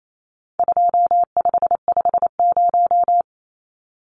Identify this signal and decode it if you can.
Morse code, heard as a keyed tone, about 28 wpm, 700 Hz. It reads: 2550
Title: Numbers - 4-Digit 28wpm